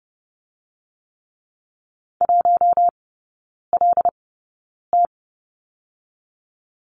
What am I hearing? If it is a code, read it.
Morse code, heard as a keyed tone, about 30 wpm, 700 Hz. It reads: 1 L T
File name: 081 Lesson 14e - Number 2 - Three Characters 30wpm